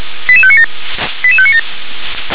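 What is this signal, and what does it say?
Five-tone selective call (ZVEI), heard as quick runs of five tones, tones of about 70 ms. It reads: 88597 88597
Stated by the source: ZVEI five-tone selective call received on an SDR.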